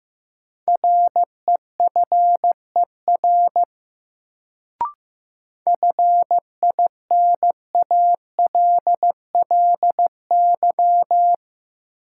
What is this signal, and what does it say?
Morse code, heard as a keyed tone, about 16 wpm, 700 Hz. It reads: REFER FINALLY
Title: Sentences from Top 1000 Words - New Word Drill - 15wpm